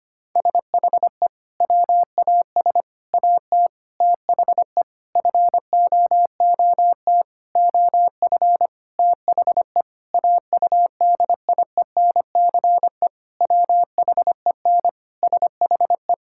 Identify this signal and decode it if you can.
Morse code, heard as a keyed tone, about 25 wpm, 700 Hz. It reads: SHE WAS AT THE FOOT OF THE AUDIENCE WHEN SHE
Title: Sentences from Top 1000 Words - Review Part 5 - 25wpm